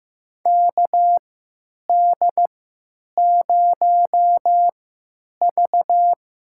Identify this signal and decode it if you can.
Morse code, heard as a keyed tone, about 15 wpm, spaced as for 9 wpm, 700 Hz. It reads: KD0V
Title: USA Call Signs - 6 Characters - 3x Character Spacing - 15wpm